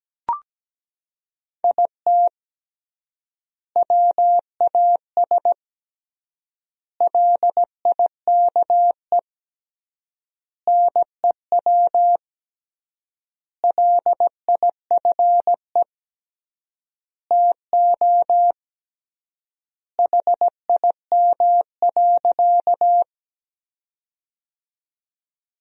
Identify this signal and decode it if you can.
Morse code, heard as a keyed tone, about 17 wpm, 700 Hz. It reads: IT WAS LIKE NEW LIFE TO HIM.
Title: 3x Word-Spacing - Sentences from Top 300 Words - 17wpm